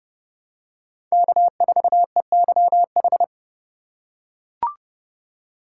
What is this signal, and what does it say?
Morse code, heard as a keyed tone, about 30 wpm, 700 Hz. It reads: K4EYH
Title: USA Call Signs - 4x Character Spacing - 30wpm